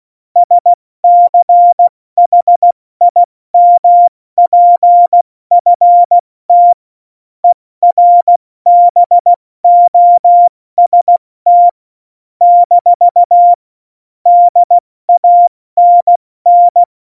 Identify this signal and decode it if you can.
Morse code, heard as a keyed tone, about 16 wpm, 700 Hz. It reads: SCHIMPFT ERBOST - DANN
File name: Morsekurs / Witze / Witz 20. November 2018, 70 F80